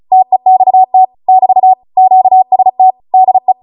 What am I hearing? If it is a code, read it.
Morse code, heard as a keyed tone, about 36 wpm, 750 Hz. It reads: TEXT <BT> QST DE